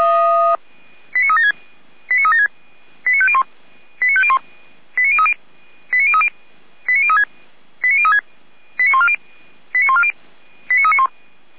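Five-tone selective call (ZVEI), heard as quick runs of five tones, tones of about 70 ms. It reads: 89376 89376 89501 89501 89030 89030 89036 89140 89140 89391
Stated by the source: ZVEI five-tone selective call, address 89042, received on an SDR.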